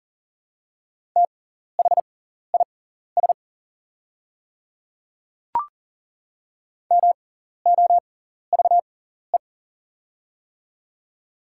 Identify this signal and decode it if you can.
Morse code, heard as a keyed tone, about 40 wpm, spaced as for 14 wpm, 700 Hz. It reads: THIS MOVE
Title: 6x Character Spacing - Top 300 Words - 40wpm